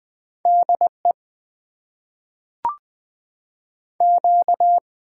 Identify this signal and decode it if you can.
Morse code, heard as a keyed tone, about 20 wpm, 700 Hz. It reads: DE Q
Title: Top 90 QSO Elements - 6x Character Spacing - 20wpm